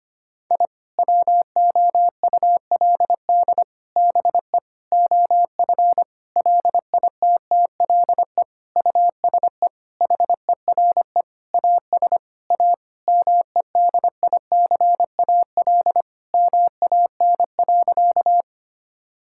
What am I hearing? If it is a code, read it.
Morse code, heard as a keyed tone, about 25 wpm, 700 Hz. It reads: I WOULD BE OF LITTLE USE HERE AS A MEDICAL MAN.
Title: Sentences from Top 1000 Words - Review Part 5 - 25wpm